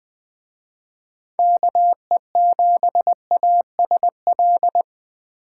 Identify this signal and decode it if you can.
Morse code, heard as a keyed tone, about 21 wpm, 700 Hz. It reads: KE7ASL
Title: Speed Racing - USA 2x3 Call Signs 17wpm